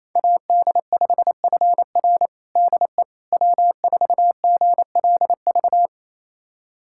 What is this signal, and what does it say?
Morse code, heard as a keyed tone, about 28 wpm, 700 Hz. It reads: AD5FR DE W4GLV
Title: USA Callsign DE Callsign - 5 Characters - 28wpm